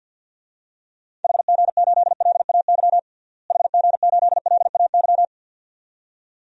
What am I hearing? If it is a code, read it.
Morse code, heard as a keyed tone, about 51 wpm, 700 Hz. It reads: HG8LAY HG8LAY
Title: DXCC-100 Call Signs - 6 Characters - Repeated 2x - 50wpm